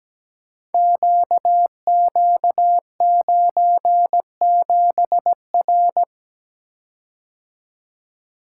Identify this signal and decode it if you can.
Morse code, heard as a keyed tone, about 17 wpm, 700 Hz. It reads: QQ97R
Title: Code Groups - 5 Letter:Numbers - 17wpm